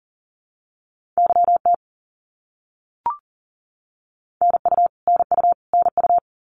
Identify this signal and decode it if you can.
Morse code, heard as a keyed tone, about 39 wpm, 700 Hz. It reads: YT NU NU NU
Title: Canadian Provinces and Territories - Repeated 3x - 40wpm